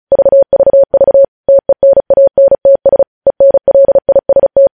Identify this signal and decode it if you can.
Morse code, heard as a keyed tone, about 36 wpm, 550 Hz. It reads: VVV TENANTS ENLIST
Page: news headlines in morse code at 35 wpm